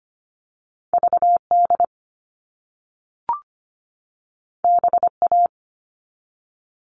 Morse code, heard as a keyed tone, about 25 wpm, 700 Hz. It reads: VD BA
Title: All Cut Numbers - 2-Digit 25wpm